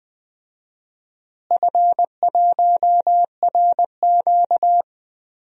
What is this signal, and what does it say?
Morse code, heard as a keyed tone, about 20 wpm, 700 Hz. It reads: F1RQ
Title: DXCC-100 Call Signs - 4 Characters - Repeated 2x - 20wpm